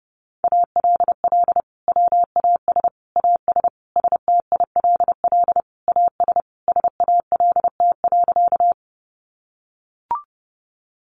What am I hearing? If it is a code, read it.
Morse code, heard as a keyed tone, about 30 wpm, 700 Hz. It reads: ALL WAS AS STILL AS SALT.
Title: Sentences from Top 800 Words 30wpm